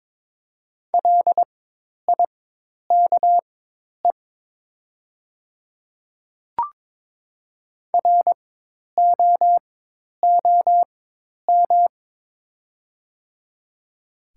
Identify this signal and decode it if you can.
Morse code, heard as a keyed tone, about 22 wpm, spaced as for 10 wpm, 700 Hz. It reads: LIKE ROOM
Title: 4x Character Spacing - Top 500 Words - 22wpm